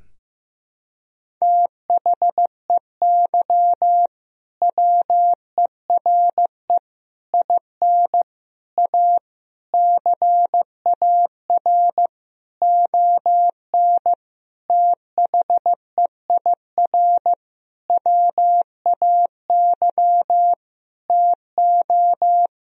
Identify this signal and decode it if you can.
Morse code, heard as a keyed tone, about 15 wpm, 700 Hz. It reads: THEY WERE IN A CAR ON THEIR WAY TO
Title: Sentences from Top 1000 Words - Review Part 1 - 15wpm